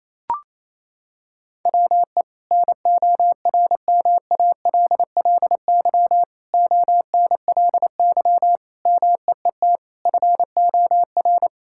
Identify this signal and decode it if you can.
Morse code, heard as a keyed tone, about 28 wpm, 700 Hz. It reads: WE NORMALLY ONLY MEET FOR
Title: Sentences from Top 1000 Words - Review Part 6 - 28wpm